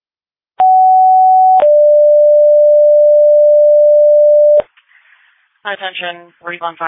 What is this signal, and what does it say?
Two-tone page, heard as a one-second tone, then a three-second tone, about 750 then 590 Hz, with speech. It is heard from 0.6 to 4.6 s.
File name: Readlyn Fire